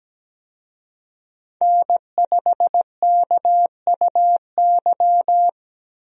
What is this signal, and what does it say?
Morse code, heard as a keyed tone, about 17 wpm, 700 Hz. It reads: N5KUY
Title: USA Call Signs - 5 Characters - Repeated 3x - 17wpm